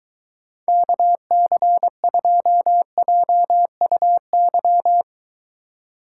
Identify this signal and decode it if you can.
Morse code, heard as a keyed tone, about 23 wpm, 700 Hz. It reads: KC2JUY